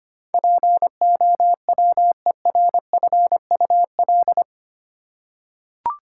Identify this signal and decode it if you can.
Morse code, heard as a keyed tone, about 25 wpm, 700 Hz. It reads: POWERFUL
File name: Eight Letter Words - 25wpm